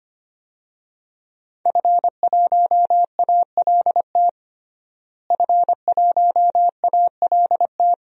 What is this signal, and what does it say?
Morse code, heard as a keyed tone, about 25 wpm, 700 Hz. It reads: F1ALT F1ALT
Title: DXCC-100 Call Signs - 5 Characters - Repeated 3x - 25wpm